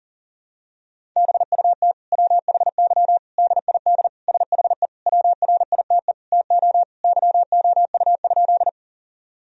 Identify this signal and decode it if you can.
Morse code, heard as a keyed tone, about 40 wpm, 700 Hz. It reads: BUT WHY DID SHE WRITE TO YOU?
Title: Sentences from Top 1000 Words - Review Part 1 - 40wpm